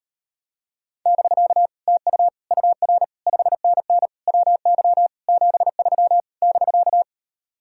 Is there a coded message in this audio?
Yes: Morse code from 1.1 to 7.0 s, heard as a keyed tone, about 38 wpm, 700 Hz. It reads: <BK> TU UR 5NN WY 73 <BK>